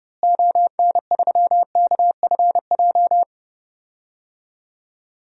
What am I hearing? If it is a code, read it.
Morse code, heard as a keyed tone, about 31 wpm, 700 Hz. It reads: ON3KFJ
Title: DXCC-100 Call Signs - 6 Characters - Repeated 3x - 30wpm